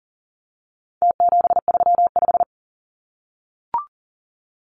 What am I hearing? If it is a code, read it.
Morse code, heard as a keyed tone, about 40 wpm, 700 Hz. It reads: T735
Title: Numbers - Contest Style - 4-Digit 40wpm